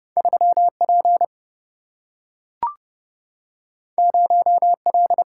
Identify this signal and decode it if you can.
Morse code, heard as a keyed tone, about 30 wpm, 700 Hz. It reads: 3 P 0 L
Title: Single-Letter Single-Number 30wpm